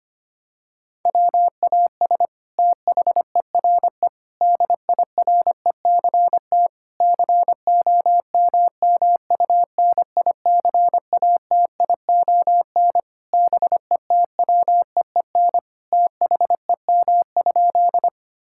Morse code, heard as a keyed tone, about 25 wpm, 700 Hz. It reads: WAS THERE DIRECT COMMUNICATION BETWEEN THEM?
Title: Sentences from Top 1000 Words - Review Part 6 - 25wpm